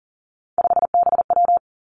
Morse code, heard as a keyed tone, about 41 wpm, 700 Hz. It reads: 5BW
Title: Three Challenging Letter-Number Combinations 40wpm